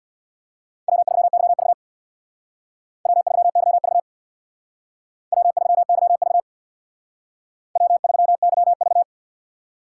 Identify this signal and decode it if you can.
Morse code, heard as a keyed tone, about 62 wpm, 700 Hz. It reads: W3YV W3YV W3YV W3YV
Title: Speed Racing - USA Call Signs 50wpm